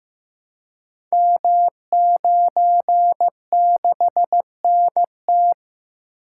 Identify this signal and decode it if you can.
Morse code, heard as a keyed tone, about 15 wpm, 700 Hz. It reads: M96NT
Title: Code Groups - 5 Letter:Numbers - 15wpm